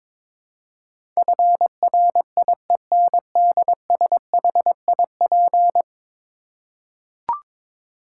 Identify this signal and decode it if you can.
Morse code, heard as a keyed tone, about 21 wpm, 700 Hz. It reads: FRIENDSHIP
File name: practice / 22wpm / 10 Letter Words - First 3 Letters Given - 22wpm